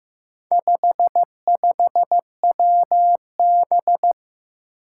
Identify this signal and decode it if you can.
Morse code, heard as a keyed tone, about 15 wpm, 700 Hz. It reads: 55WB
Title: Four Challenging Letter-Number Combinations 15wpm